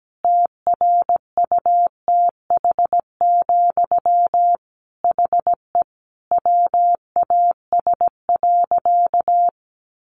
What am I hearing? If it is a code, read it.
Morse code, heard as a keyed tone, about 17 wpm, 700 Hz. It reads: TRUTH, HE WAS.
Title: Sentences from Top 800 Words 17wpm